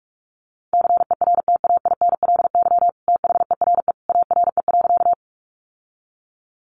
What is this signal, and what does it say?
Morse code, heard as a keyed tone, about 44 wpm, 700 Hz. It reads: CERTAINLY THERE ARE.
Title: Sentences from Top 900 Words 45wpm